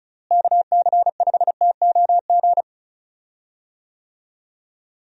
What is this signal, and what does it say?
Morse code, heard as a keyed tone, about 36 wpm, 700 Hz. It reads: KC5TOG
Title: USA Call Signs - 6 Characters - Repeated 2x - 35wpm